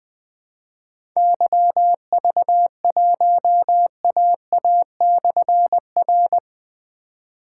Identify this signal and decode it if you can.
Morse code, heard as a keyed tone, about 20 wpm, 700 Hz. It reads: YV1AA/R
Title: Difficult Call Signs 20wpm